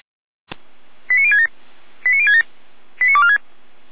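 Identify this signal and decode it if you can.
Five-tone selective call (ZVEI), heard as quick runs of five tones, tones of about 70 ms. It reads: 89067 89067 89246
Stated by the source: ZVEI five-tone selective call received by SDR, fire brigade Orlamünde (feuerwehr-orlamuende) server.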